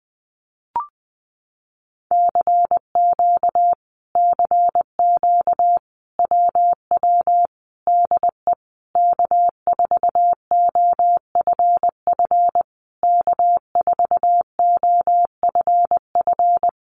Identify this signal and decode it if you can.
Morse code, heard as a keyed tone, about 20 wpm, 700 Hz. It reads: CQ CQ WW DE K4OFF K4OFF
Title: Common QSO Phrases Encore 20wpm